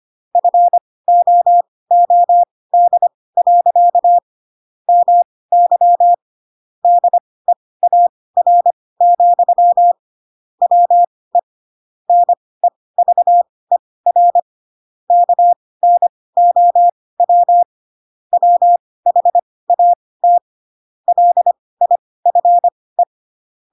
Morse code, heard as a keyed tone, about 25 wpm, spaced as for 18 wpm, 700 Hz. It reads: FOOD. MY DEAR, WE NEVER KNOW WHAT LIFE
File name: Pinocchio / Pinocchio ch7 12 25 ik1hge